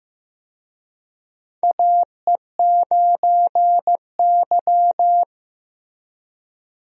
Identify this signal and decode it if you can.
Morse code, heard as a keyed tone, about 15 wpm, 700 Hz. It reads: AE9Y
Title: USA Call Signs - 4 Characters - Repeated 2x - 15wpm